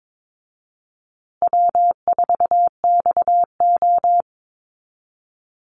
Morse code, heard as a keyed tone, about 22 wpm, 700 Hz. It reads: W4XO
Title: Easy Call Signs Repeated 3x - 22wpm